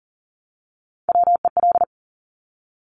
Morse code, heard as a keyed tone, about 38 wpm, 700 Hz. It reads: WEL
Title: Top Three Letter Combinations in Words 40wpm